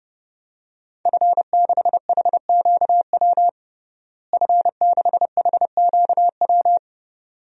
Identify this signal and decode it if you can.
Morse code, heard as a keyed tone, about 30 wpm, 700 Hz. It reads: F6HQW F6HQW
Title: DXCC-100 Call Signs - 5 Characters - Repeated 2x - 30wpm